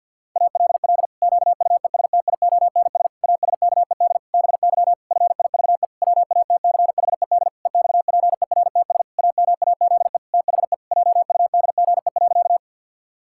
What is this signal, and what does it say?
Morse code, heard as a keyed tone, about 51 wpm, 700 Hz. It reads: ALL QUESTIONS ASKED BY FIVE WATCHED EXPERTS AMAZE THE JUDGE.
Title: Pangram Challenge v2 - 50wpm